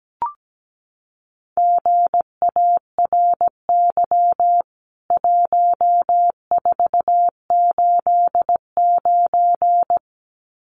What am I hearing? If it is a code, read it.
Morse code, heard as a keyed tone, about 17 wpm, 700 Hz. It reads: GARY 1489
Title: CWT Exchange 17wpm